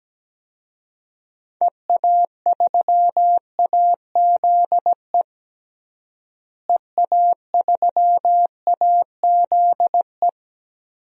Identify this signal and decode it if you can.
Morse code, heard as a keyed tone, about 17 wpm, 700 Hz. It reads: EA3AZE EA3AZE
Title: DXCC-100 Call Signs - 6 Characters - Repeated 3x - 17wpm